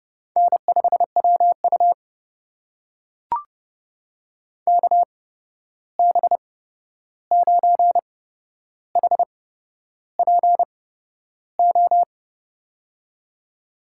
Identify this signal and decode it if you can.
Morse code, heard as a keyed tone, about 30 wpm, spaced as for 11 wpm, 700 Hz. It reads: N5WU KB9HPO
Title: USA Call Signs - 8x Character Spacing - 30wpm